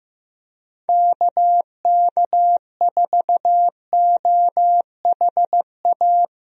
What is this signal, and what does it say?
Morse code, heard as a keyed tone, about 15 wpm, 700 Hz. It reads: KK4OHA
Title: USA Call Signs - 6 Characters - Repeated 3x - 15wpm